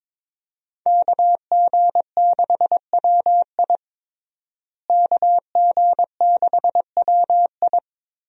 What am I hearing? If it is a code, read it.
Morse code, heard as a keyed tone, about 22 wpm, 700 Hz. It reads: KG6WI KG6WI
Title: USA Call Signs - 5 Characters - Repeated 3x - 22wpm